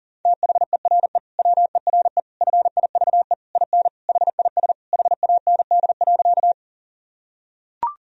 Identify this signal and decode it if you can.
Morse code, heard as a keyed tone, about 40 wpm, 700 Hz. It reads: THERE WERE FIVE IN HIS HAND.
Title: Sentences from Top 1000 Words - Review Part 2 - 40wpm